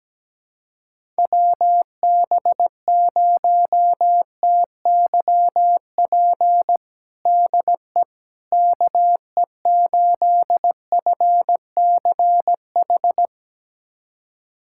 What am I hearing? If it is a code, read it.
Morse code, heard as a keyed tone, about 17 wpm, 700 Hz. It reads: WB0TYP DE KE8FCH
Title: USA Callsign DE Callsign - 6 Characters - 17wpm